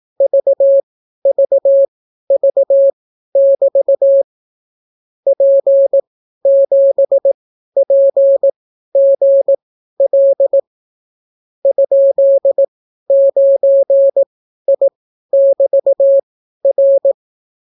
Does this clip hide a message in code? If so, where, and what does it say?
Yes: Morse code from 0.2 to 17.1 s, heard as a keyed tone, about 18 wpm, spaced as for 12 wpm, 550 Hz. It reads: VVV<BT> P7PGL ?9I<BT>R